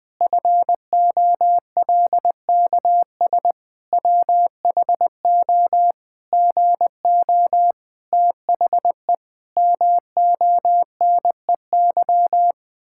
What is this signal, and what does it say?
Morse code, heard as a keyed tone, about 20 wpm, 700 Hz. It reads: FOLKS WHO GO THE MONEY
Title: Pangram Challenge v2 - 20wpm